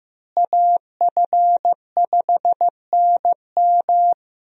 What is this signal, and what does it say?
Morse code, heard as a keyed tone, about 15 wpm, 700 Hz. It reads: AF5NM
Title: USA Call Signs - 5 Characters - Repeated 2x - 15wpm